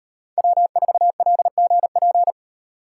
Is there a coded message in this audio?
Yes: Morse code from 0.4 to 2.3 s, heard as a keyed tone, about 36 wpm, 700 Hz. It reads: W4LGP